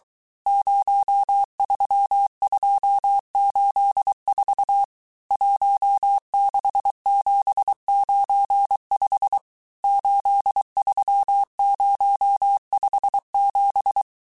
Morse code, heard as a keyed tone, about 23 wpm, 800 Hz. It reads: 03284 16795 83057
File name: cours-du-mercredi / audios / cours-855-texte-chiffre-vitesse-22-mpm